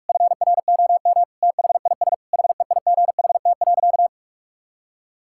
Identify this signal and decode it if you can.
Morse code, heard as a keyed tone, about 46 wpm, 700 Hz. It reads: FROM THIS HEIGHT.
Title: Sentences from Top 1600 Words 45wpm